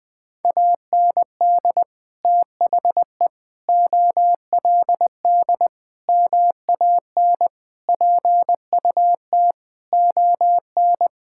Morse code, heard as a keyed tone, about 21 wpm, 700 Hz. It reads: AND THE OLD MAN PUT ON
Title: Sentences from Top 1600 Words 20wpm